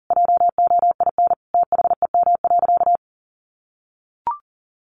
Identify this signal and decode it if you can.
Morse code, heard as a keyed tone, about 40 wpm, 700 Hz. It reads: JOIN THEM.
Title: Sentences from Top 1400 Words 40wpm